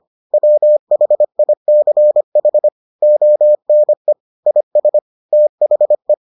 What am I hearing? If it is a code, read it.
Morse code, heard as a keyed tone, about 25 wpm, 600 Hz. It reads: WHICH ONE IS THE